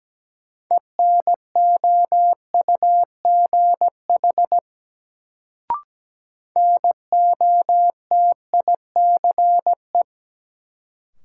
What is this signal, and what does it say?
Morse code, heard as a keyed tone, about 17 wpm, 700 Hz. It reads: ENOUGH NOTICE
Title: Sentences from Top 400 Words - New Word Drill - 17wpm